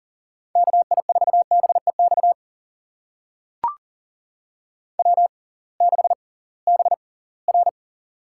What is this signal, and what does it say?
Morse code, heard as a keyed tone, about 40 wpm, spaced as for 23 wpm, 700 Hz. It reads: KI4BEX W6BR
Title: USA Call Signs - 6x Character Spacing - 40wpm